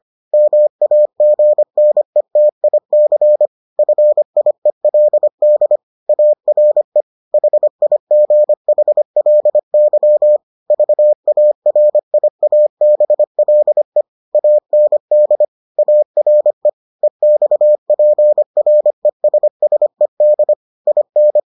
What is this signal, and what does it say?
Morse code, heard as a keyed tone, about 25 wpm, 600 Hz. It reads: MAGNETIC FIELD ARE HIGHLY VARIABLE AND ARE EXPRESSED IN